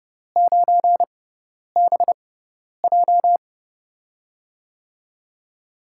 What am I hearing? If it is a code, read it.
Morse code, heard as a keyed tone, about 30 wpm, spaced as for 10 wpm, 700 Hz. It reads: 9BJ
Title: USA Call Signs - 5 Characters - 6x Character Spacing - 30wpm